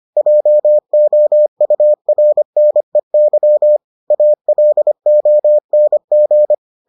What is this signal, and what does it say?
Morse code, heard as a keyed tone, about 25 wpm, 600 Hz. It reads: JOURNEY ALONG